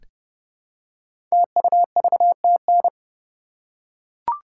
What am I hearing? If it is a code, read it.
Morse code, heard as a keyed tone, about 30 wpm, 700 Hz. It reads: TUVTN
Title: Code Groups - 5 Letters - 30wpm